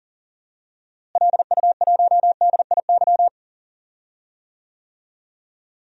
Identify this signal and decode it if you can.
Morse code, heard as a keyed tone, about 40 wpm, 700 Hz. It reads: LU1DIY
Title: DXCC-100 Call Signs - 6 Characters - Repeated 3x - 40wpm